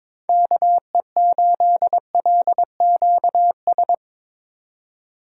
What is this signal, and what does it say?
Morse code, heard as a keyed tone, about 24 wpm, 700 Hz. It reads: KE8LQS
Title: Speed Racing - USA 2x3 Call Signs 22wpm